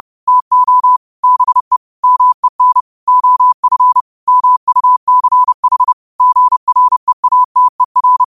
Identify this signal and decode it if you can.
Morse code, heard as a keyed tone, about 30 wpm, 1000 Hz. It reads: TO BE MEN OF MUCH GREATER
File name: qotdfiles / 30wpm